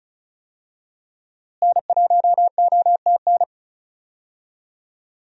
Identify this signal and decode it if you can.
Morse code, heard as a keyed tone, about 34 wpm, 700 Hz. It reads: N1OTN
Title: USA Call Signs - 5 Characters - Repeated 2x - 35wpm